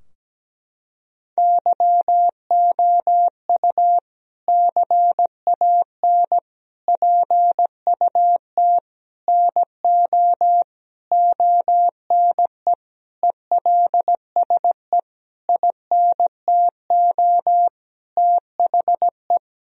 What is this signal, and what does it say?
Morse code, heard as a keyed tone, about 17 wpm, 700 Hz. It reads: YOU CAN PUT NO ONE ELSE INTO THE
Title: Sentences from Top 1000 Words - Review Part 2 - 17wpm